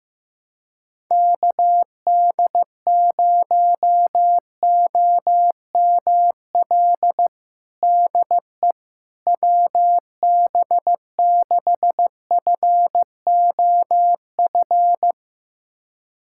Morse code, heard as a keyed tone, about 15 wpm, 700 Hz. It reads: KD0OML DE WB6FOF
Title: USA Callsign DE Callsign 15wpm